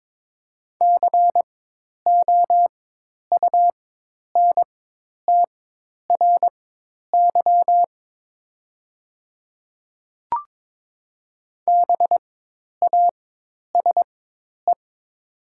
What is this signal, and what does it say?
Morse code, heard as a keyed tone, about 22 wpm, spaced as for 10 wpm, 700 Hz. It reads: COUNTRY BASE
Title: 4x Character Spacing - Top 300 Words - 22wpm